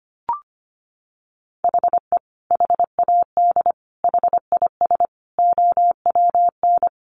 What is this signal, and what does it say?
Morse code, heard as a keyed tone, about 25 wpm, 700 Hz. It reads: HE HAD HIS OWN
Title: Sentences from Top 1200 Words 25wpm